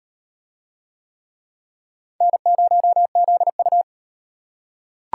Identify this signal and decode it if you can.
Morse code, heard as a keyed tone, about 38 wpm, 700 Hz. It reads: N0ZU